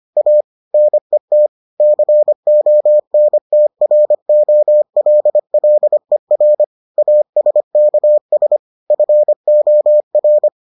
Morse code, heard as a keyed tone, about 25 wpm, 600 Hz. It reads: A NET CONTROLLER ASKS FOR